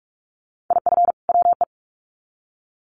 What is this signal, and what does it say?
Morse code, heard as a keyed tone, about 44 wpm, 700 Hz. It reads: IF WE
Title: Sets of 2 Words 45wpm